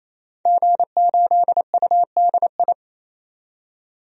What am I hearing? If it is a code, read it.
Morse code, heard as a keyed tone, about 28 wpm, 700 Hz. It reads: G8UDI
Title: DXCC-100 Call Signs - 5 Characters - 28wpm